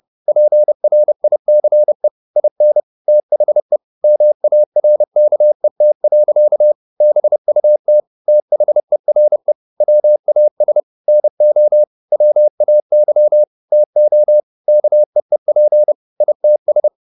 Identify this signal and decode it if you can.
Morse code, heard as a keyed tone, about 30 wpm, 600 Hz. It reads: PRICE IN THE MARKET. BUT THERE WAS NO WAY TO KEEP ITS